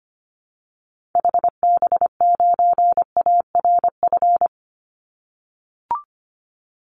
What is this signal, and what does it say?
Morse code, heard as a keyed tone, about 25 wpm, 700 Hz. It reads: HB9ARF
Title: Call Signs - 25wpm